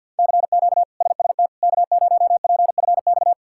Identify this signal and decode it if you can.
Morse code, heard as a keyed tone, about 47 wpm, 700 Hz. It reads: CQ SST K0PFX
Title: K1USN SST QSO Practice - 50wpm